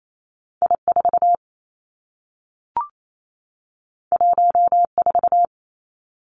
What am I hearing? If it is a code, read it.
Morse code, heard as a keyed tone, about 28 wpm, 700 Hz. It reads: I 4 1 4
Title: Two Challenging Letter-Number Combinations 28wpm